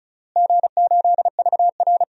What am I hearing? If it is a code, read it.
Morse code, heard as a keyed tone, about 34 wpm, 700 Hz. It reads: G8VR
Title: DXCC-100 Call Signs - 4 Characters - Repeated 2x - 35wpm